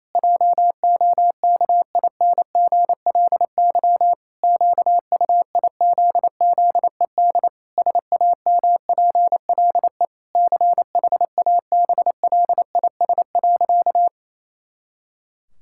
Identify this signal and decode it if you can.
Morse code, heard as a keyed tone, about 28 wpm, 700 Hz. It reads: JOKINGLY QUIZZED SAMPLE CHABLIS.
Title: Pangram Challenge v2 - 28wpm